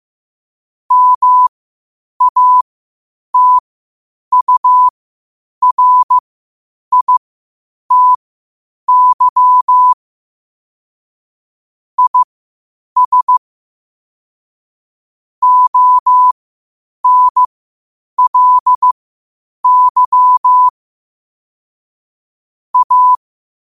Morse code, heard as a keyed tone, about 15 wpm, spaced as for 8 wpm, 1000 Hz. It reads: MATURITY IS ONLY A
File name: qotdfiles / 8wpm